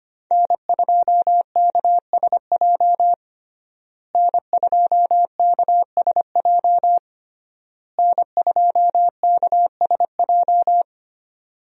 Morse code, heard as a keyed tone, about 25 wpm, 700 Hz. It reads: N2KSJ N2KSJ N2KSJ
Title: USA Call Signs - 5 Characters - Repeated 3x - 25wpm